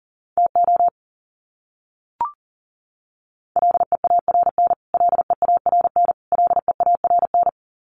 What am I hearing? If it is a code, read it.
Morse code, heard as a keyed tone, about 40 wpm, 700 Hz. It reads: TO LEARN LEARN LEARN
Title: Top 500 Words Repeated 40wpm